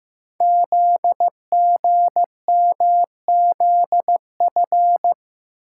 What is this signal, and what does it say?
Morse code, heard as a keyed tone, about 15 wpm, 700 Hz. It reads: ZGMZF
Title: Code Groups - 5 Letters - 15wpm